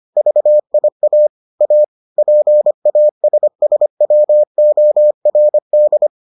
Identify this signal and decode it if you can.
Morse code, heard as a keyed tone, about 25 wpm, 600 Hz. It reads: VIA A PASSWORD